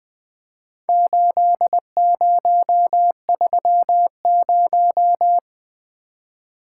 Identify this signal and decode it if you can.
Morse code, heard as a keyed tone, about 21 wpm, 700 Hz. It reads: 8030
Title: Four Letter-Number Combinations - 20wpm